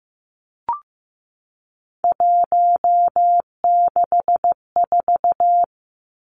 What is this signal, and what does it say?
Morse code, heard as a keyed tone, about 15 wpm, 700 Hz. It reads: 164
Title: Numbers - 3-Digit 15wpm